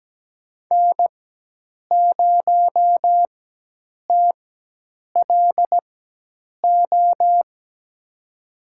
Morse code, heard as a keyed tone, about 17 wpm, spaced as for 8 wpm, 700 Hz. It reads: N0TLO
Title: USA Call Signs - 5 Characters - 4x Character Spacing - 17wpm